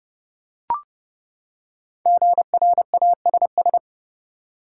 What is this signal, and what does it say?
Morse code, heard as a keyed tone, about 31 wpm, 700 Hz. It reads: GRASS
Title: Five Letter Words - 30wpm